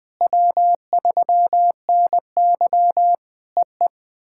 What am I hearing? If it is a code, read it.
Morse code, heard as a keyed tone, about 21 wpm, 700 Hz. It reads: W3NY EE
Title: POTA QSO Practice - 20wpm